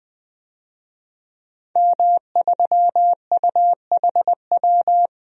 Morse code, heard as a keyed tone, about 20 wpm, 700 Hz. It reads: M3UHW
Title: DXCC-100 Call Signs - 5 Characters - 20wpm